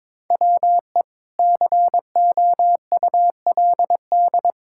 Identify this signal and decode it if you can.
Morse code, heard as a keyed tone, about 22 wpm, 700 Hz. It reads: WE COULD
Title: Pangram Challenge v2 - 22wpm